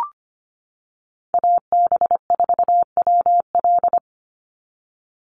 Morse code, heard as a keyed tone, about 25 wpm, 700 Hz. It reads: AB4WL
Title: NA SOTA Chaser Callsigns 25wpm